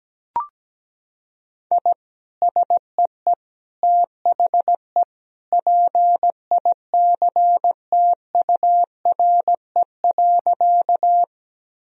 Morse code, heard as a keyed tone, about 17 wpm, 700 Hz. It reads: I SEE THE PICTURE.
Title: Sentences from Top 1000 Words - Review Part 1 - 17wpm